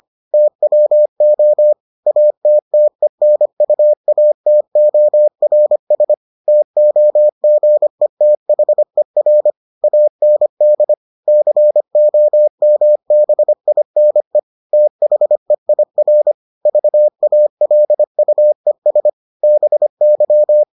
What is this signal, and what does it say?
Morse code, heard as a keyed tone, about 25 wpm, 600 Hz. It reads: TWO ATTENUATORS TOGETHER AND COMBINE THEIR VALUES BY